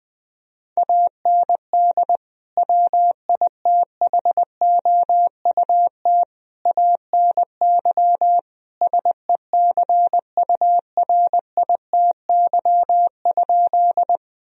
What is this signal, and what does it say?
Morse code, heard as a keyed tone, about 21 wpm, 700 Hz. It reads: AND WITHOUT ANY SECURITY?
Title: Sentences from Top 1000 Words - Review Part 4 - 20wpm